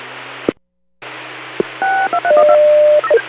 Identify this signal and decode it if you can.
DTMF, heard as telephone touch tones, tones of about 80 ms. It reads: 623